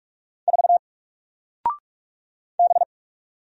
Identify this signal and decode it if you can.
Morse code, heard as a keyed tone, about 44 wpm, 700 Hz. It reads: 4 B
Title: Single Character (Letter:Number:Punctuation:Prosign) Alt 45wpm